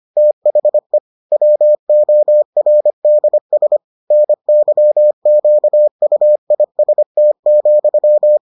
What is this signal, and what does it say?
Morse code, heard as a keyed tone, about 25 wpm, 600 Hz. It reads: THE WORDS NYQUIST,